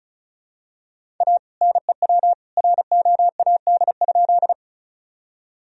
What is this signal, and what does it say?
Morse code, heard as a keyed tone, about 34 wpm, 700 Hz. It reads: A NEW ROAD?
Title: Sentences from Top 1000 Words - Review Part 2 - 35wpm